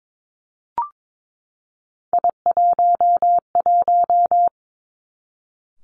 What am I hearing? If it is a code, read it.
Morse code, heard as a keyed tone, about 22 wpm, 700 Hz. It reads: I11
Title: Three Challenging Letter-Number Combinations 22wpm